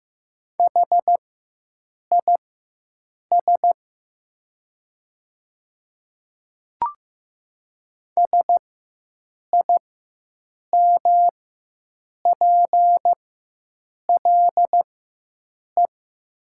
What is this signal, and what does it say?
Morse code, heard as a keyed tone, about 15 wpm, spaced as for 7 wpm, 700 Hz. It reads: HIS SIMPLE
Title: 4x Character Spacing - Top 500 Words - 15wpm